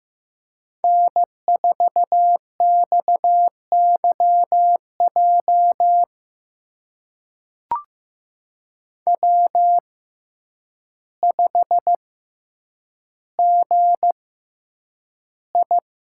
Morse code, heard as a keyed tone, about 15 wpm, spaced as for 8 wpm, 700 Hz. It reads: N4XYJ W5GI
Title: USA Call Signs - 6x Character Spacing - 15wpm